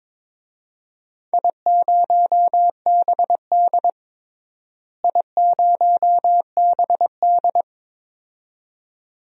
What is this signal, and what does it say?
Morse code, heard as a keyed tone, about 22 wpm, 700 Hz. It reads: I0BD I0BD
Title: DXCC-100 Call Signs - 4 Characters - Repeated 2x - 22wpm